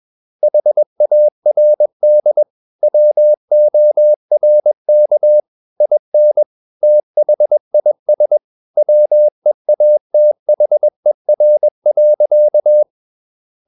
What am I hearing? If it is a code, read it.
Morse code, heard as a keyed tone, about 21 wpm, 600 Hz. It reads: HARD WORK IN THIS WEATHER.
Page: CW med Gnister